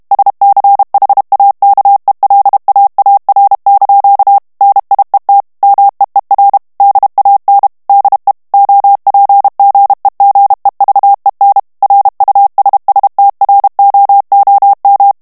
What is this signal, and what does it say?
Morse code, heard as a keyed tone, about 32 wpm, 800 Hz. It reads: SCHAKELAAR) NIET MEER DAN DE OPGEGEVEN RUSSTROOM
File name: HOLLANDAIS texte clair 2 vitesse 30 mots minute